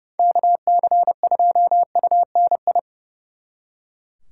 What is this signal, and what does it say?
Morse code, heard as a keyed tone, about 38 wpm, 700 Hz. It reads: KC2UNI